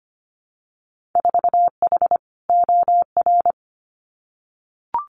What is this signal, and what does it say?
Morse code, heard as a keyed tone, about 25 wpm, 700 Hz. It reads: 4H OR
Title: Winter Field Day Exchanges Encore 25wpm